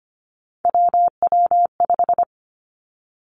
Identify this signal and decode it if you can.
Morse code, heard as a keyed tone, about 25 wpm, 700 Hz. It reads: WW5
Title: Three Challenging Letter-Number Combinations 25wpm